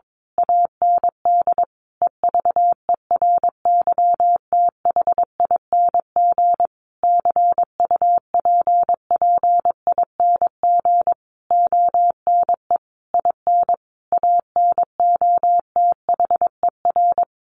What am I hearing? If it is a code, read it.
Morse code, heard as a keyed tone, about 22 wpm, 700 Hz. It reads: AND EVERYTHING CUPPING ONE IN ANOTHER
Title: A Sound Of Thunder 22wpm